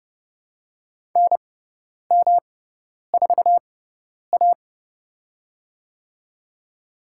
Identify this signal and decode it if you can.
Morse code, heard as a keyed tone, about 30 wpm, spaced as for 10 wpm, 690 Hz. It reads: NM4A
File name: Lesson 18g - Letter m - Call Signs 30wpm